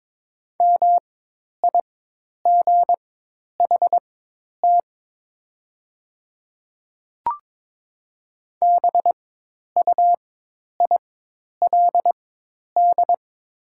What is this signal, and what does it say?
Morse code, heard as a keyed tone, about 22 wpm, spaced as for 10 wpm, 700 Hz. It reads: MIGHT BUILD
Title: 4x Character Spacing - Top 300 Words - 22wpm